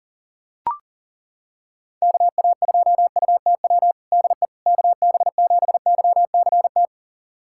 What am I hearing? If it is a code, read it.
Morse code, heard as a keyed tone, about 40 wpm, 700 Hz. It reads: KA2UTW DE KB7YCT
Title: USA Callsign DE Callsign 40wpm